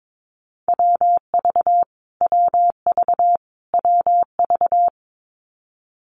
Morse code, heard as a keyed tone, about 22 wpm, 700 Hz. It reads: WV WV WV
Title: US States Repeated 3x 22wpm